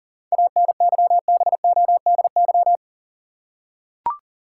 Morse code, heard as a keyed tone, about 37 wpm, 700 Hz. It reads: ANYBODY
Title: Seven Letter Words - 40wpm